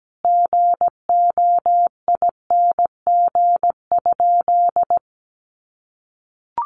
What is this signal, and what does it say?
Morse code, heard as a keyed tone, about 17 wpm, 700 Hz. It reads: GOING?
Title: Sentences from Top 1100 Words 17wpm